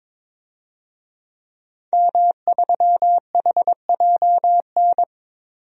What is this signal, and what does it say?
Morse code, heard as a keyed tone, about 22 wpm, 700 Hz. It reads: M3HJN
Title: DXCC-100 Call Signs - 5 Characters - 22wpm